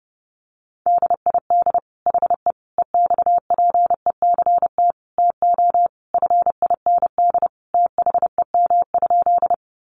Morse code, heard as a keyed tone, about 30 wpm, 700 Hz. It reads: DID HE EXPECT TO FIND THEM?
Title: Sentences from Top 1100 Words 30wpm